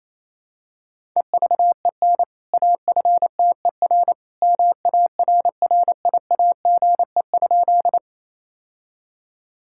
Morse code, heard as a keyed tone, about 28 wpm, 700 Hz. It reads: EVEN AFTER MARRIAGE?
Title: Sentences from Top 1000 Words - Review Part 5 - 28wpm